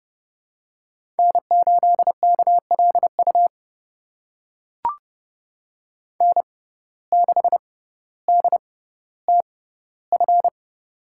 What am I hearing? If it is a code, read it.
Morse code, heard as a keyed tone, about 30 wpm, spaced as for 15 wpm, 700 Hz. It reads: N8KLU N6DTF
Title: USA Call Signs - 5 Characters - 6x Character Spacing - 30wpm